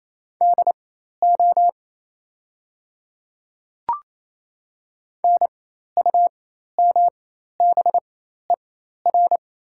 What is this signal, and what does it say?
Morse code, heard as a keyed tone, about 28 wpm, spaced as for 13 wpm, 700 Hz. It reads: DO NUMBER
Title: 4x Character Spacing - Top 100 Words - 28wpm